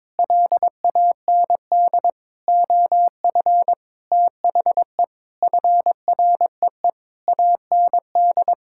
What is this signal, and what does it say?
Morse code, heard as a keyed tone, about 22 wpm, 700 Hz. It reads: LAND OF THE FREE AND
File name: practice / 22wpm / Sentences with 8-Word Phrases - 22wpm